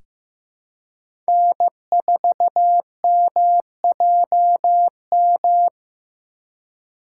Morse code, heard as a keyed tone, about 15 wpm, 700 Hz. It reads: N4MJM
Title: USA Call Signs - 5 Characters - 3x Character Spacing - 15wpm